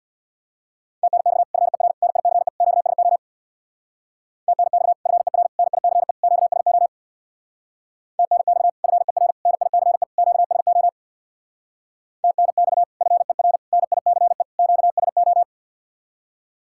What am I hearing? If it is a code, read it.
Morse code, heard as a keyed tone, about 66 wpm, 700 Hz. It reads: TNX FER NICE QSO TNX FER NICE QSO TNX FER NICE QSO TNX FER NICE QSO